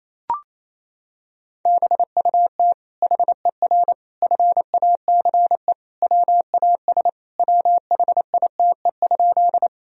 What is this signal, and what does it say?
Morse code, heard as a keyed tone, about 28 wpm, 700 Hz. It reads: BUT HER FACE WAS WHITE?
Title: Sentences from Top 1000 Words - Review Part 2 - 28wpm